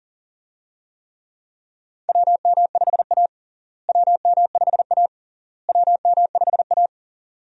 Morse code, heard as a keyed tone, about 40 wpm, 700 Hz. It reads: WM5A WM5A WM5A
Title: USA Call Signs - 4 Characters - Repeated 3x - 40wpm